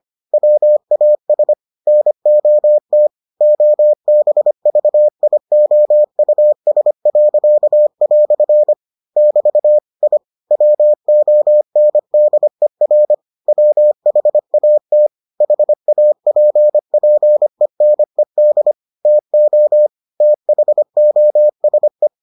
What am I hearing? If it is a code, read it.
Morse code, heard as a keyed tone, about 25 wpm, 600 Hz. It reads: WAS NOT OBVIOUS." <BT> I WONDER WHAT HAPPENED TO THOSE